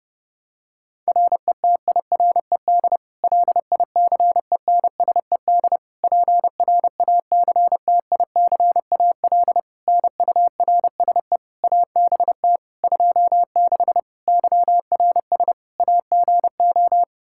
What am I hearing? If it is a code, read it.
Morse code, heard as a keyed tone, about 30 wpm, 700 Hz. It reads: RETIRED LICENSED PRACTICAL NURSE ABT 26 YRS AGO
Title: Practice QSO - Exchange 3 - More Info - 30wpm